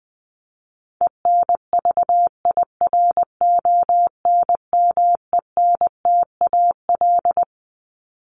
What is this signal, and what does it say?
Morse code, heard as a keyed tone, about 20 wpm, 700 Hz. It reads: ENVIRONMENTAL
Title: Top 1001-1250 Words 20wpm